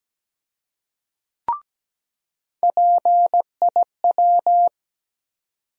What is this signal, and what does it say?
Morse code, heard as a keyed tone, about 17 wpm, 700 Hz. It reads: PIW
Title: USA Call Sign Postfixes - 3 Letters 17wpm